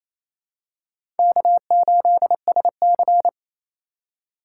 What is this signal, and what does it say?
Morse code, heard as a keyed tone, about 28 wpm, 700 Hz. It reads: K8SC
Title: USA Call Signs - 4 Characters - 2x Character Spacing - 28wpm